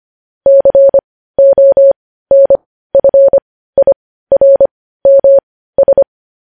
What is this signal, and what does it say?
Morse code, heard as a keyed tone, about 25 wpm, spaced as for 15 wpm, 550 Hz. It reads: CONFIRMS
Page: news-headlines-in-morse-code-at-15-wpm